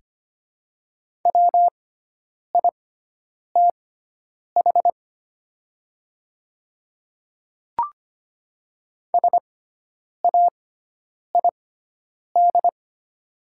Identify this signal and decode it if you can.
Morse code, heard as a keyed tone, about 25 wpm, spaced as for 9 wpm, 690 Hz. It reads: WITH SAID
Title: 6x Character Spacing - Top 100 Words - 25wpm